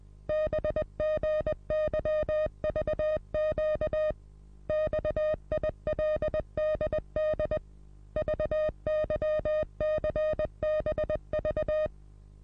Morse code, heard as a keyed tone, about 20 wpm, 620 Hz. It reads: BGYVQ XILDD VYCBV
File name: repos / Morsekurs / DJ4UF Morsen27